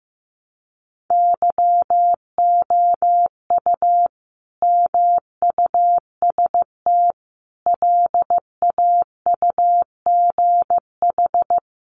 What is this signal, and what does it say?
Morse code, heard as a keyed tone, about 15 wpm, 700 Hz. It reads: YOU MUST LAUGH
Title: Sentences from Top 600 Words 15wpm